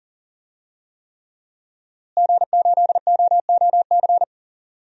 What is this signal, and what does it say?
Morse code, heard as a keyed tone, about 40 wpm, 700 Hz. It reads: G8OOC
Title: DXCC-100 Call Signs - 5 Characters - Repeated 3x - 40wpm